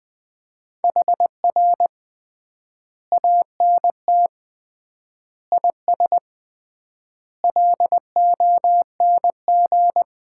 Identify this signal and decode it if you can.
Morse code, heard as a keyed tone, about 20 wpm, 700 Hz. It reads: HR ANT IS LONG
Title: Common QSO Phrases - 3x Word-Spacing - 20wpm